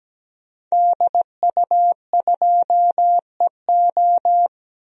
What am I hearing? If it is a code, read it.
Morse code, heard as a keyed tone, about 17 wpm, 700 Hz. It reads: DU2EO
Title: DXCC-100 Call Signs - 5 Characters - Repeated 3x - 17wpm